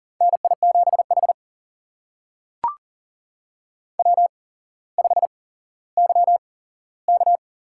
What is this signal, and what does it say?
Morse code, heard as a keyed tone, about 41 wpm, spaced as for 17 wpm, 700 Hz. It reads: NI7H W5YK
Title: USA Call Signs - 8x Character Spacing - 40wpm